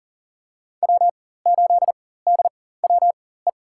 Morse code, heard as a keyed tone, about 40 wpm, spaced as for 19 wpm, 700 Hz. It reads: W8DWE
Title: USA Call Signs - 4x Character Spacing - 40wpm